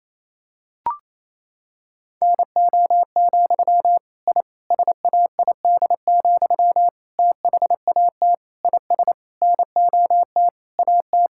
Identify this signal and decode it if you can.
Morse code, heard as a keyed tone, about 28 wpm, 700 Hz. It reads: NO, I SAID, THAT IS NOT AT
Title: Sentences from Top 1000 Words - Review Part 6 - 28wpm